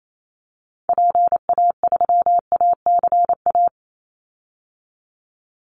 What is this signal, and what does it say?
Morse code, heard as a keyed tone, about 28 wpm, 700 Hz. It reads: PA3ACA
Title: Call Signs - 28wpm